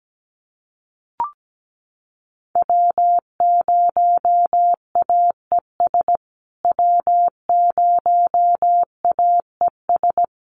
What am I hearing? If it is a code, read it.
Morse code, heard as a keyed tone, about 17 wpm, 700 Hz. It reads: W0AES W0AES
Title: USA Call Signs Repeated 3x 17wpm